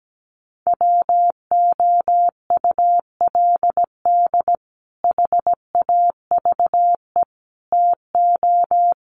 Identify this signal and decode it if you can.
Morse code, heard as a keyed tone, about 17 wpm, 700 Hz. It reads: WOULD HAVE TO
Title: Sets of 4 Words 17wpm